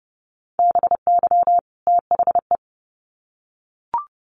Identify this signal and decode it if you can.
Morse code, heard as a keyed tone, about 29 wpm, 700 Hz. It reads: BY THE
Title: Sets of 2 Words 30wpm